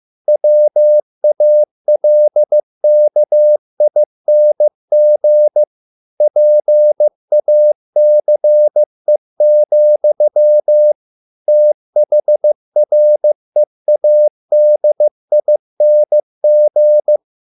Morse code, heard as a keyed tone, about 15 wpm, 600 Hz. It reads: WALKING PACE, THREADING